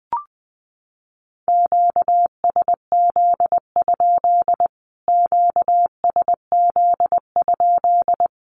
Morse code, heard as a keyed tone, about 20 wpm, 700 Hz. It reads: QSZ? QSZ?
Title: Q-Codes Repeated 3x 20wpm